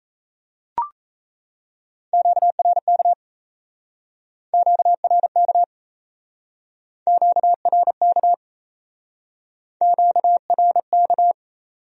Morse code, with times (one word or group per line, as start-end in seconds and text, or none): 2.1-3.1 QRK
4.5-5.6 QRK
7.1-8.3 QRK
9.8-11.3 QRK